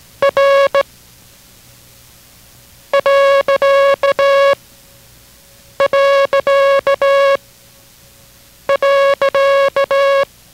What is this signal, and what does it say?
Morse code, heard as a keyed tone, about 17 wpm, 550 Hz, with noise. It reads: R . .